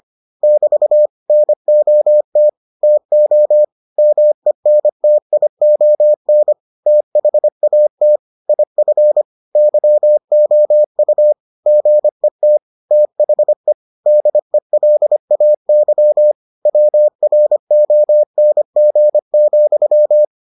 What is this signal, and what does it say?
Morse code, heard as a keyed tone, about 25 wpm, 600 Hz. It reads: <BT> NOT TO MENTION THAT IF YOU GET THE DELAY WRONG,